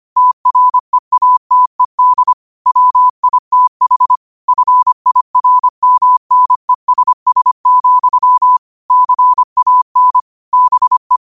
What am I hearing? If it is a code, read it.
Morse code, heard as a keyed tone, about 25 wpm, 1000 Hz. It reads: TREATED WITH FIRMNESS, CAN BE
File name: qotdfiles / 25wpm